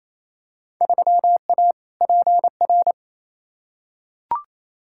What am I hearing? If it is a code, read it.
Morse code, heard as a keyed tone, about 28 wpm, 700 Hz. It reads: 3A PR
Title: Field Day Exchanges Encore - 28wpm